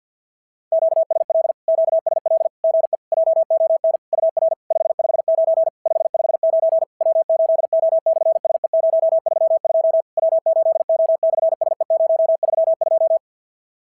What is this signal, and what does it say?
Morse code, heard as a keyed tone, about 50 wpm, 650 Hz. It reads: QSL QSL GE JON UR 559 559 W8O/SE032 W8O/SE032
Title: SOTA QSO Practice 50wpm